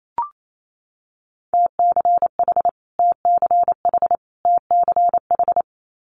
Morse code, heard as a keyed tone, about 28 wpm, 700 Hz. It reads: TCH TCH TCH
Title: Multi-letter Phonemes Repeated 3x 28wpm